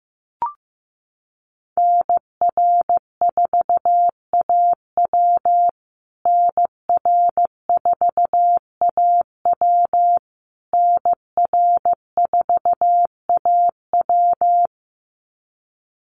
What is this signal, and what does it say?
Morse code, heard as a keyed tone, about 15 wpm, 700 Hz. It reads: NR4AW NR4AW NR4AW
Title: USA Call Signs Repeated 3x 15wpm